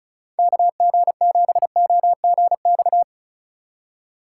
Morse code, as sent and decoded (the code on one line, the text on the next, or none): -.- --. --... --- --. -..-
KG7OGX